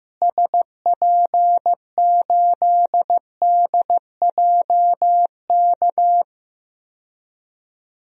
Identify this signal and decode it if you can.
Morse code, heard as a keyed tone, about 16 wpm, 700 Hz. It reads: SP8DJK
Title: DXCC-100 Call Signs - 6 Characters - Repeated 3x - 15wpm